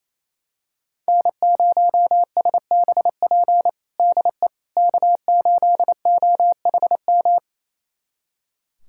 Morse code, heard as a keyed tone, about 28 wpm, 700 Hz. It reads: N0SBP DE K8OHM
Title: USA Callsign DE Callsign - 5 Characters - 28wpm